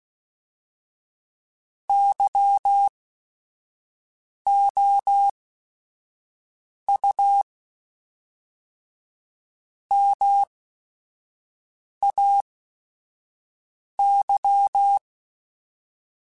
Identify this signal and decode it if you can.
Morse code, heard as a keyed tone, about 16 wpm, spaced as for 5 wpm, 800 Hz. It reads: YOU MAY
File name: Chuckles / Chuckles-mp3s / SoftWords @5